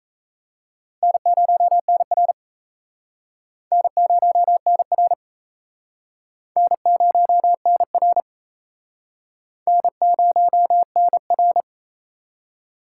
Morse code, heard as a keyed tone, about 34 wpm, 700 Hz. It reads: N0NR N0NR N0NR N0NR
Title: Speed Racing - USA Call Signs 28wpm